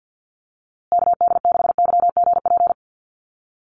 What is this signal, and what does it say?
Morse code, heard as a keyed tone, about 50 wpm, 700 Hz. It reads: KD6YGP
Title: USA Call Signs Repeated 3x 50wpm